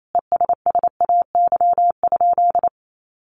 Morse code, heard as a keyed tone, about 28 wpm, 700 Hz. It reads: ESSAY?
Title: Sentences from Top 1500 Words 28wpm